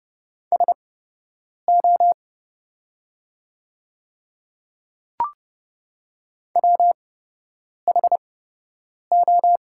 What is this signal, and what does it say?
Morse code, heard as a keyed tone, about 30 wpm, spaced as for 8 wpm, 700 Hz. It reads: SO WHO
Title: 8x Character Spacing - Top 100 Words - 30wpm